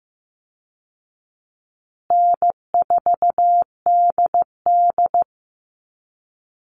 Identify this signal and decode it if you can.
Morse code, heard as a keyed tone, about 15 wpm, 700 Hz. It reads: N4DD
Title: Call Signs Repeated 2x - 15wpm